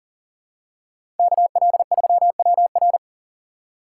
Morse code, heard as a keyed tone, about 40 wpm, 700 Hz. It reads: KL3WR
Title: USA Call Signs - 5 Characters - 3x Character Spacing - 40wpm